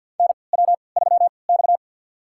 Morse code, heard as a keyed tone, about 53 wpm, spaced as for 29 wpm, 700 Hz. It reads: NW3X
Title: USA Call Signs - 4 Characters - 3x Character Spacing - 50wpm